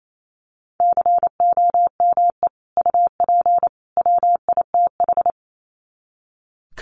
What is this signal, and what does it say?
Morse code, heard as a keyed tone, about 28 wpm, 700 Hz. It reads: COME UP WITH
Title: Sets of 3 Words 28wpm